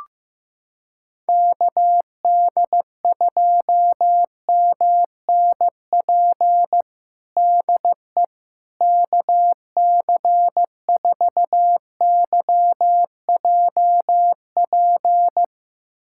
Morse code, heard as a keyed tone, about 15 wpm, 700 Hz. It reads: KD2MNP DE KC4YJP
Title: USA Callsign DE Callsign 15wpm